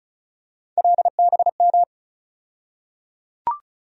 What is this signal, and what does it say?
Morse code, heard as a keyed tone, about 36 wpm, 700 Hz. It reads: LBM
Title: USA Call Sign Postfixes - 3 Letters 35wpm